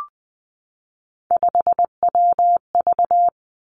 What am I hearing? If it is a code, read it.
Morse code, heard as a keyed tone, about 20 wpm, 700 Hz. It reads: 5WV
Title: Three Challenging Letter-Number Combinations 20wpm